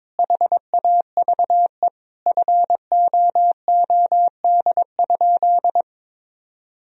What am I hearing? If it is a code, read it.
Morse code, heard as a keyed tone, about 22 wpm, 700 Hz. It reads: HAVE FOOD?
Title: Sentences from Top 1000 Words - Review Part 1 - 22wpm